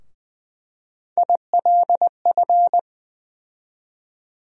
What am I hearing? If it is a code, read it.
Morse code, heard as a keyed tone, about 21 wpm, 700 Hz. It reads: ILF
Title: USA Call Sign Postfixes - 3 Letters 20wpm